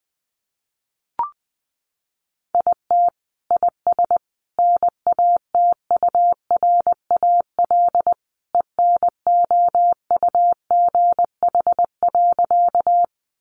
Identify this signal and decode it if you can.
Morse code, heard as a keyed tone, about 20 wpm, 700 Hz. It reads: IT IS NATURAL ENOUGH.
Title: Sentences from Top 700 Words 20wpm